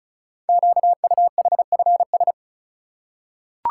Morse code, heard as a keyed tone, about 36 wpm, 700 Hz. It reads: QUHFS
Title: Code Groups - 5 Letters - 35wpm